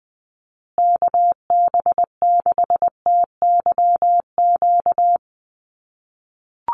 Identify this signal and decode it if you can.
Morse code, heard as a keyed tone, about 20 wpm, 700 Hz. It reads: KB6TYQ
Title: USA Call Signs Repeated 2x 20wpm